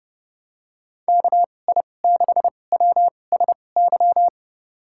Morse code, heard as a keyed tone, about 30 wpm, spaced as for 22 wpm, 700 Hz. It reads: KI6WSY
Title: USA Call Signs - 6 Characters - 2x Character Spacing - 30wpm